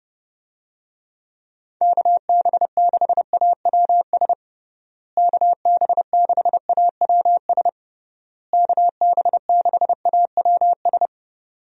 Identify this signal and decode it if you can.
Morse code, heard as a keyed tone, about 31 wpm, 700 Hz. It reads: KB6AWS KB6AWS KB6AWS
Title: USA Call Signs - 6 Characters - Repeated 3x - 30wpm